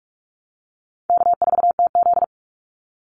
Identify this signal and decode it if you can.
Morse code, heard as a keyed tone, about 44 wpm, 700 Hz. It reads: K4TZ
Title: Easy Call Signs - 45wpm